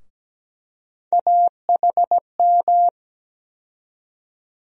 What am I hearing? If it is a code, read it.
Morse code, heard as a keyed tone, about 17 wpm, 700 Hz. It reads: AHM
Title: USA Call Sign Postfixes - 3 Letters 17wpm